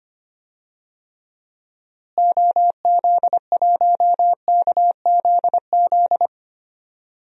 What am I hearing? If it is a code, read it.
Morse code, heard as a keyed tone, about 25 wpm, 700 Hz. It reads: OZ1KZZ
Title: DXCC-100 Call Signs - 6 Characters - Repeated 2x - 25wpm